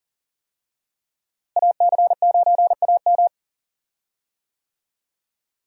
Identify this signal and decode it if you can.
Morse code, heard as a keyed tone, about 40 wpm, 700 Hz. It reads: AC9AM
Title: USA Call Signs - 5 Characters - Repeated 2x - 40wpm